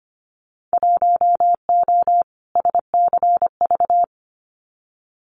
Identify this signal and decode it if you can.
Morse code, heard as a keyed tone, about 25 wpm, 700 Hz. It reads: 1O SCV
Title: Winter Field Day Exchanges 25wpm